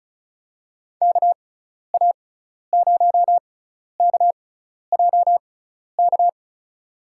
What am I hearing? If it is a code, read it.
Morse code, heard as a keyed tone, about 34 wpm, spaced as for 12 wpm, 700 Hz. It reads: KA0KJK
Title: USA Call Signs - 6 Characters - 6x Character Spacing - 35wpm